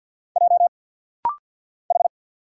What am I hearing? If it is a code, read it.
Morse code, heard as a keyed tone, about 50 wpm, 700 Hz. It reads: J H
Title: Single Letters - Rapid-Fire 50wpm